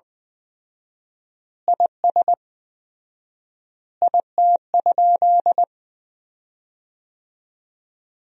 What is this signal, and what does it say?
Morse code, heard as a keyed tone, about 21 wpm, 700 Hz. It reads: IS IT?
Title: 4x Word-Spacing - Sentences from Top 100 Words - 20wpm